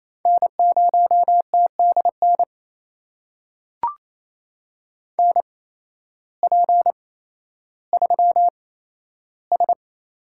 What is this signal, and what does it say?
Morse code, heard as a keyed tone, about 28 wpm, spaced as for 13 wpm, 700 Hz. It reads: N0TDN NP3S
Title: USA Call Signs - 5 Characters - 8x Character Spacing - 28wpm